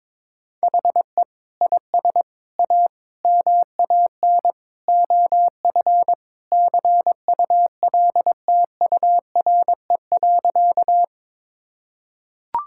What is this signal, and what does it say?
Morse code, heard as a keyed tone, about 22 wpm, 700 Hz. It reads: HE IS A MAN OF CULTURE.
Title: Sentences from Top 1000 Words - Review Part 6 - 22wpm